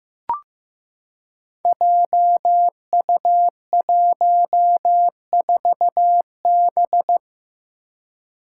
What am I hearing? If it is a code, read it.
Morse code, heard as a keyed tone, about 15 wpm, 700 Hz. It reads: JU14B
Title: Five Challenging Letter-Number Combinations 15wpm